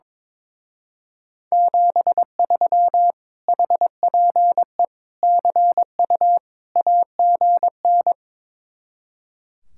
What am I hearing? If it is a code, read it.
Morse code, heard as a keyed tone, about 22 wpm, 700 Hz. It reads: 73 HPE CU AGN
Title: Speed-Racing Common QSO Phrases - 22wpm